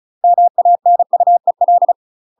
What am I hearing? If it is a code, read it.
Morse code, heard as a keyed tone, about 36 wpm, 700 Hz. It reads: MANUEL